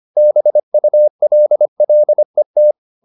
Morse code, heard as a keyed tone, about 25 wpm, 600 Hz. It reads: BULLET
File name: the old man and the sea part 1 25wpm